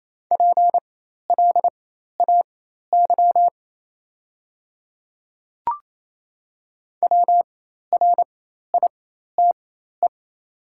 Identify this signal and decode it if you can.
Morse code, heard as a keyed tone, about 28 wpm, spaced as for 13 wpm, 700 Hz. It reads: PLAY WRITE
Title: 4x Character Spacing - Top 300 Words - 28wpm